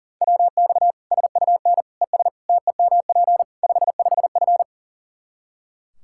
Morse code, heard as a keyed tone, about 40 wpm, 700 Hz. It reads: WX SUN ES TEMP 55F
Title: K1USN SST QSO Practice 40wpm